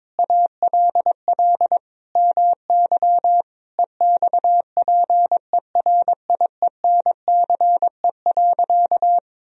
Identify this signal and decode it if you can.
Morse code, heard as a keyed tone, about 22 wpm, 700 Hz. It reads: ALL MY EXPERIENCE.
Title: Sentences from Top 1000 Words - Review Part 3 - 22wpm